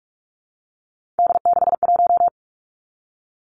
Morse code, heard as a keyed tone, about 46 wpm, 700 Hz. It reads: D61
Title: Three Challenging Letter-Number Combinations 45wpm